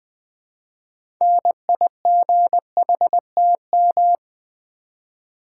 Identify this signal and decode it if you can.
Morse code, heard as a keyed tone, about 20 wpm, 700 Hz. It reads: NIGHTM
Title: Build A Word - 3 Letters At A Time - 9-Letter Words - 20wpm